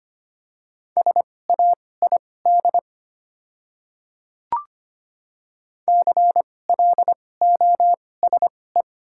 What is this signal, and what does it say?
Morse code, heard as a keyed tone, about 25 wpm, spaced as for 18 wpm, 700 Hz. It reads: SAID CLOSE
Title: 2x Character Spacing - Top 300 Words - 25wpm